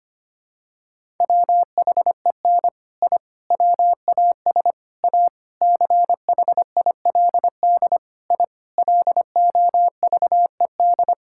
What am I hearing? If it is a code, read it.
Morse code, heard as a keyed tone, about 25 wpm, 700 Hz. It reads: WHEN I WAS A CHILD I LOVED
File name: practice / 25wpm / Sentences with 3-Word Phrases - 25wpm